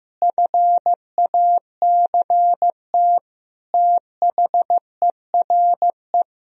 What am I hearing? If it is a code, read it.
Morse code, heard as a keyed tone, about 15 wpm, 700 Hz. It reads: FACT THERE
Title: Sentences from Top 1000 Words - Review Part 3 - 15wpm